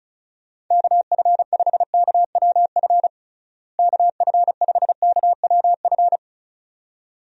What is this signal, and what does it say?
Morse code, heard as a keyed tone, about 34 wpm, 700 Hz. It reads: KF5KWF KF5KWF
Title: USA Call Signs - 6 Characters - Repeated 2x - 35wpm